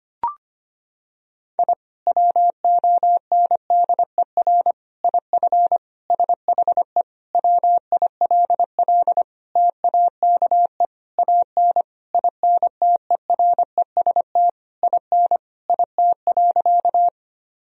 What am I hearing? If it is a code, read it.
Morse code, heard as a keyed tone, about 25 wpm, 700 Hz. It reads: I WONDER IF SHE WILL TAKE AN INTEREST IN IT.
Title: Sentences from Top 1000 Words - Review Part 3 - 25wpm